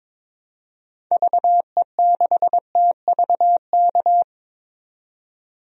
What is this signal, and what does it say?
Morse code, heard as a keyed tone, about 22 wpm, 700 Hz. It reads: VE6TVK
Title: DXCC-100 Call Signs - 6 Characters - 22wpm